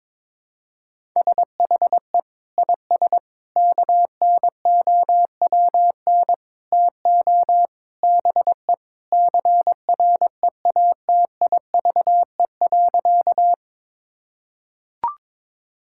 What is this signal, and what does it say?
Morse code, heard as a keyed tone, about 22 wpm, 700 Hz. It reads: SHE IS KNOWN TO BE CREATIVE.
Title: Sentences from Top 1000 Words - Review Part 5 - 22wpm